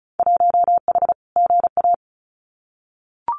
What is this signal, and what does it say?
Morse code, heard as a keyed tone, about 36 wpm, 700 Hz. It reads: 1H GA
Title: Winter Field Day Exchanges Encore 35wpm